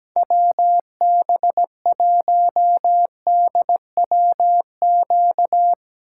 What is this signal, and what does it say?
Morse code, heard as a keyed tone, about 17 wpm, 700 Hz. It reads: WB1DWQ
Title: USA Call Signs - 6 Characters - 4x Character Spacing - 17wpm